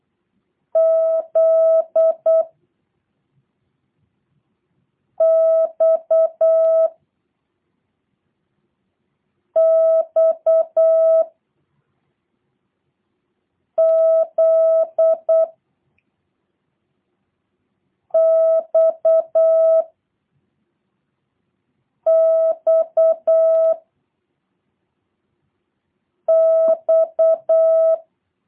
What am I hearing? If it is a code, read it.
Morse code, heard as a keyed tone, about 8 wpm, 650 Hz. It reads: Z X X Z X X X